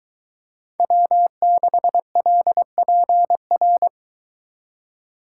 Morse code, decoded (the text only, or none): W6LPR